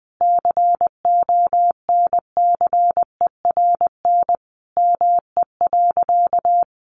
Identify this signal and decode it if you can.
Morse code, heard as a keyed tone, about 20 wpm, 700 Hz. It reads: CONCERN ME.
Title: Sentences from Top 1000 Words 20wpm